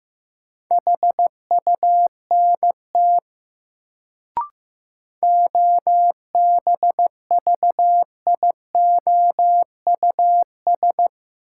Morse code, heard as a keyed tone, about 15 wpm, 700 Hz. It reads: HUNT OBVIOUS
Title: Sentences from Top 1600 Words - New Word Drill Alt - 15wpm